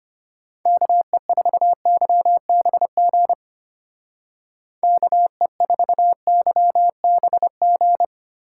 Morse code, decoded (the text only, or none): KE4YBG KE4YBG